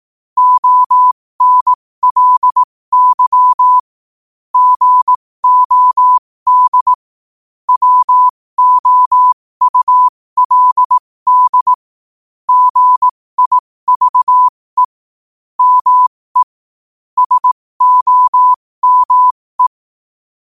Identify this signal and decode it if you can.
Morse code, heard as a keyed tone, about 18 wpm, spaced as for 15 wpm, 1000 Hz. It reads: ONLY GOD WOULD GIVE ME SOME
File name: qotdfiles / 15wpm